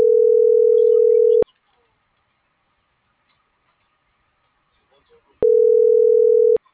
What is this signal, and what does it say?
Ringback tone (North America), 440 + 480 Hz.